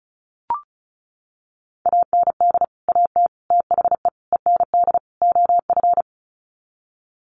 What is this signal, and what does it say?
Morse code, heard as a keyed tone, about 34 wpm, 700 Hz. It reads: AND AT THE END OF
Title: Sets of 5 Words 35wpm